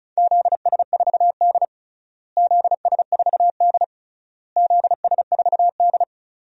Morse code, heard as a keyed tone, about 36 wpm, 700 Hz. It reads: ZS4D ZS4D ZS4D
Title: DXCC-100 Call Signs - 4 Characters - Repeated 3x - 35wpm